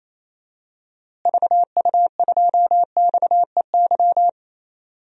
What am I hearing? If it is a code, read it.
Morse code, heard as a keyed tone, about 28 wpm, 700 Hz. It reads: VU2XEY
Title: DXCC-100 Call Signs - 6 Characters - Repeated 2x - 28wpm